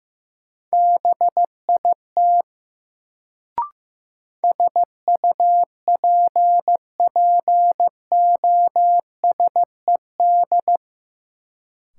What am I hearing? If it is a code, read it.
Morse code, heard as a keyed tone, about 15 wpm, 700 Hz. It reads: BIT SUPPOSED
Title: Sets of 3 Words - New Word Drill - 15wpm